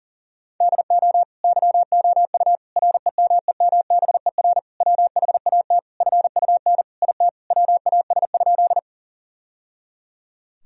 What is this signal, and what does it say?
Morse code, heard as a keyed tone, about 40 wpm, 700 Hz. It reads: DO YOU REMEMBER WHAT FUN IT WAS?
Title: Sentences from Top 1000 Words - Review Part 4 - 40wpm